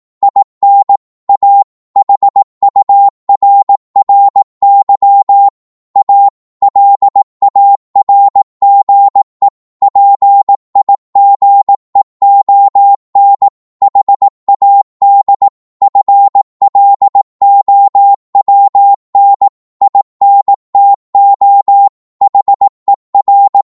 Morse code, heard as a keyed tone, about 18 wpm, 800 Hz. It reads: IN A HURRY A LARGE PIGEON HAD FLOWN INTO HER
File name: 18wpm